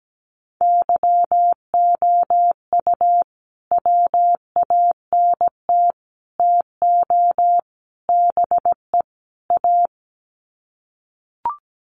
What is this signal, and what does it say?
Morse code, heard as a keyed tone, about 17 wpm, 700 Hz. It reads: YOU WANT TO BE A
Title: Sets of 5 Words Encore2 17wpm